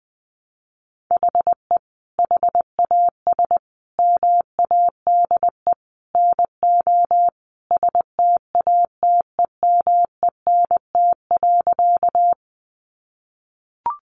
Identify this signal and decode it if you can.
Morse code, heard as a keyed tone, about 20 wpm, 700 Hz. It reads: HE HAS MADE NO STATEMENT.
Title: Sentences from Top 1200 Words 20wpm